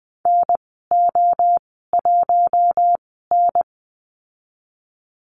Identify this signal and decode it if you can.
Morse code, heard as a keyed tone, about 20 wpm, spaced as for 15 wpm, 700 Hz. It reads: NO1N
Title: 055 Lesson 10g - Letter r - Call Signs 20wpm